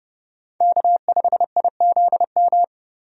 Morse code, heard as a keyed tone, about 30 wpm, 700 Hz. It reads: K5IZM
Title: USA Call Signs - 5 Characters - Repeated 3x - 30wpm